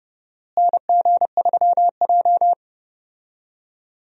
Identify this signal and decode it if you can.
Morse code, heard as a keyed tone, about 32 wpm, 700 Hz. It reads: NG3J
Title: USA Call Signs - 4 Characters - Repeated 2x - 30wpm